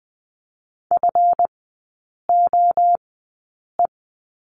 Morse code, heard as a keyed tone, about 20 wpm, spaced as for 8 wpm, 700 Hz. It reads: FOE
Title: 152 Lesson 25e - Letter p - Three Characters 20wpm